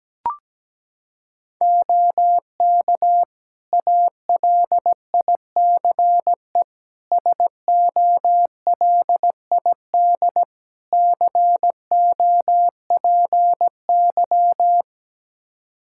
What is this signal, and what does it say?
Morse code, heard as a keyed tone, about 17 wpm, 700 Hz. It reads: OK ALICE SOLID COPY
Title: Practice QSO - Exchange 3 - More Info - 17wpm